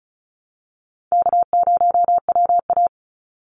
Morse code, heard as a keyed tone, about 34 wpm, 700 Hz. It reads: K0WA
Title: Call Signs - 35wpm